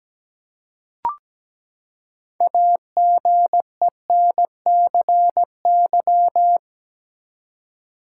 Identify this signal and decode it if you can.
Morse code, heard as a keyed tone, about 17 wpm, 700 Hz. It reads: AGENCY
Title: Six Letter Words - 17wpm